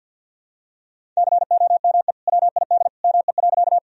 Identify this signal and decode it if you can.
Morse code, heard as a keyed tone, about 45 wpm, 700 Hz. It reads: COME WID ME.
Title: Sentences from Top 200 Words and Abbreviations 50wpm